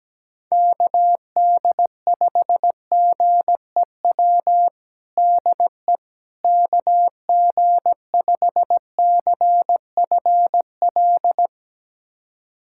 Morse code, heard as a keyed tone, about 17 wpm, 700 Hz. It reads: KD5GEW DE KG5CFL
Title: USA Callsign DE Callsign - 6 Characters - 17wpm